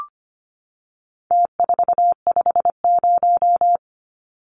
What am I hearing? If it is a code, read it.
Morse code, heard as a keyed tone, about 25 wpm, 700 Hz. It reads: T450
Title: Numbers - Contest Style - 4-Digit 25wpm